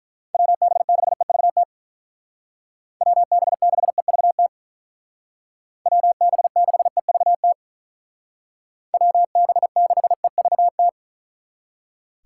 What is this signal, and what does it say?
Morse code, heard as a keyed tone, about 44 wpm, 700 Hz. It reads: WB6EVT WB6EVT WB6EVT WB6EVT